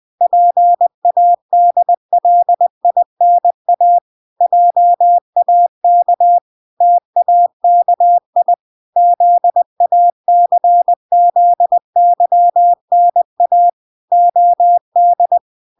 Morse code, heard as a keyed tone, about 20 wpm, 700 Hz. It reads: PADLINA JAK TAKI ZACZYNA OD